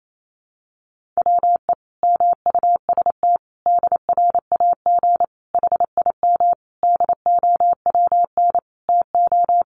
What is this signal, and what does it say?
Morse code, heard as a keyed tone, about 28 wpm, 700 Hz. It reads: WE MUST DRAG HIM DOWN TO
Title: Sentences from Top 1500 Words 28wpm